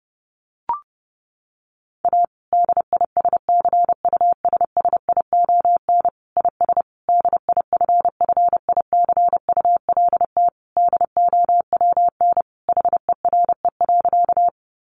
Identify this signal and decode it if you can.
Morse code, heard as a keyed tone, about 30 wpm, 700 Hz. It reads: A DISCUSSION IS DIFFICULT DOWN HERE.
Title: Sentences from Top 1100 Words 30wpm